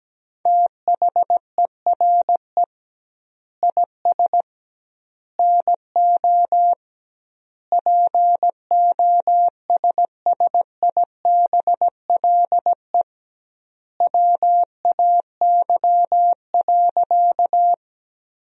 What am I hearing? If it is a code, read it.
Morse code, heard as a keyed tone, about 17 wpm, 700 Hz. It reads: THERE IS NO POSSIBLE WAY.
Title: 2x Word-Spacing - Sentences from Top 500 Words - 17wpm